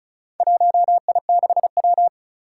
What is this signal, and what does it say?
Morse code, heard as a keyed tone, about 36 wpm, 700 Hz. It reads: 1I6W
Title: Four Challenging Letter-Number Combinations 35wpm